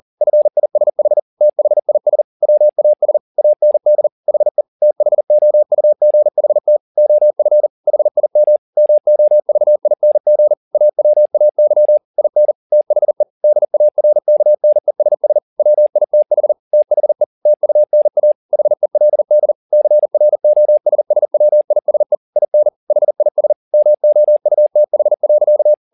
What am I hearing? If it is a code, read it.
Morse code, heard as a keyed tone, about 40 wpm, 600 Hz. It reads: FISH THIS WAS AND HE THOUGHT OF HIM MOVING AWAY IN THE DARKNESS WITH THE TUNA HELD CROSSWISE IN HIS MOUTH.